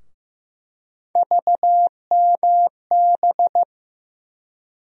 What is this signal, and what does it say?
Morse code, heard as a keyed tone, about 15 wpm, 700 Hz. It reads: VMB
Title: USA Call Sign Postfixes - 3 Letters 15wpm